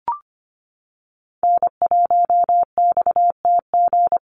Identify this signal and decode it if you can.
Morse code, heard as a keyed tone, about 25 wpm, 690 Hz. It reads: N1XTG
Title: USA Call Signs 25wpm